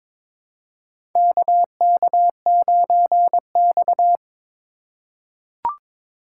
Morse code, heard as a keyed tone, about 22 wpm, 700 Hz. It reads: KK9X
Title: USA Call Signs - 4 Characters - 4x Character Spacing - 22wpm